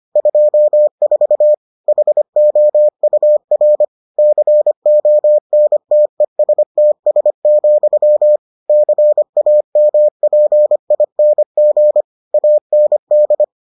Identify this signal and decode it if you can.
Morse code, heard as a keyed tone, about 25 wpm, 600 Hz. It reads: 24 HOUR CONTESTS, CAMPING AND